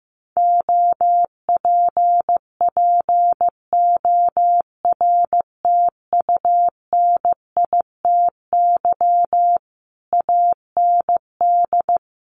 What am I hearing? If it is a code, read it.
Morse code, heard as a keyed tone, about 15 wpm, 700 Hz. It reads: OPPORTUNITY AND
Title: Sentences from Top 1200 Words 15wpm